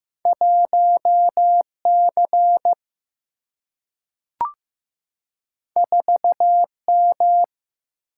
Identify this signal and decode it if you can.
Morse code, heard as a keyed tone, about 15 wpm, 700 Hz. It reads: 1 C 4 M
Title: Single-Letter Single-Number 15wpm